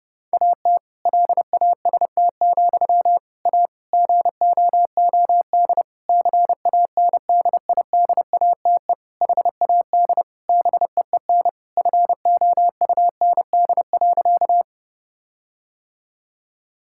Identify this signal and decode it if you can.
Morse code, heard as a keyed tone, about 30 wpm, 700 Hz. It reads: AT LAST, A GOOD CANDIDATE HAD BEEN FOUND.
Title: Sentences from Top 1600 Words 30wpm